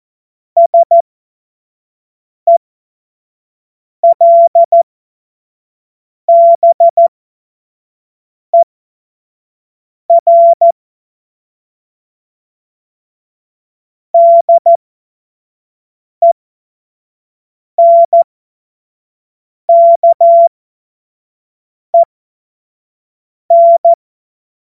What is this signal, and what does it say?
Morse code, heard as a keyed tone, about 14 wpm, spaced as for 5 wpm, 680 Hz. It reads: SELBER DENKEN
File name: Morsekurs / Witze / Witz 27. Dezember 2022 25 F70